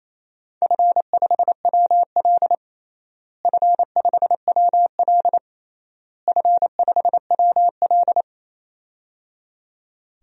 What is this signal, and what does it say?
Morse code, heard as a keyed tone, about 28 wpm, 700 Hz. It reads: F5WL F5WL F5WL
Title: DXCC-100 Call Signs - 4 Characters - Repeated 3x - 28wpm